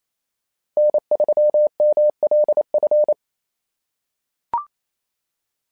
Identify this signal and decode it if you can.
Morse code, heard as a keyed tone, about 28 wpm, 600 Hz. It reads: N3MLF
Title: SOTA QSO Practice 28wpm